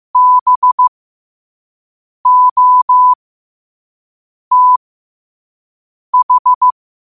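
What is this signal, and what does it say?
Morse code, heard as a keyed tone, about 15 wpm, spaced as for 5 wpm, 1000 Hz. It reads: BOTH
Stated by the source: Morse code quote of the day sent at 5 wpm.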